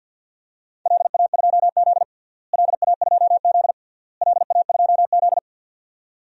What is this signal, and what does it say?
Morse code, heard as a keyed tone, about 47 wpm, 700 Hz. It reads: LA2Z LA2Z LA2Z
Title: DXCC-100 Call Signs - 4 Characters - Repeated 3x - 50wpm